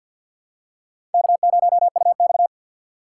Morse code, heard as a keyed tone, about 50 wpm, 700 Hz. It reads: K0UX
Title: USA Call Signs - 4 Characters - 3x Character Spacing - 50wpm